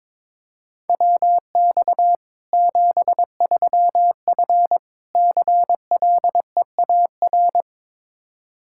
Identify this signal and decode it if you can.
Morse code, heard as a keyed tone, about 22 wpm, 700 Hz. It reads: WX 73F CLEAR
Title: Practice QSO - Exchange 2 - Info - 22wpm